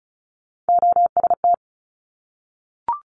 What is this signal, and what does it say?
Morse code, heard as a keyed tone, about 36 wpm, 700 Hz. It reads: OST
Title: Top Three Letter Combinations in Words 35wpm